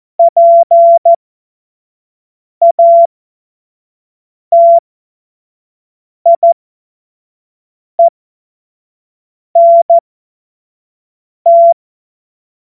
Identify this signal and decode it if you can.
Morse code, heard as a keyed tone, about 14 wpm, spaced as for 5 wpm, 680 Hz. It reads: PATIENT
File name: Morsekurs / Witze / Witz 02 April 2024 25 F70